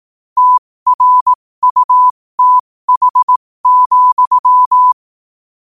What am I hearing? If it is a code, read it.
Morse code, heard as a keyed tone, about 18 wpm, spaced as for 15 wpm, 1000 Hz. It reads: TRUTH,
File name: qotdfiles / 15wpm